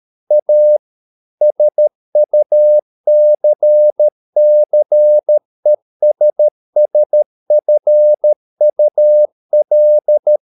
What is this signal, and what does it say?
Morse code, heard as a keyed tone, about 13 wpm, 600 Hz. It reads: A SUCCESSFUL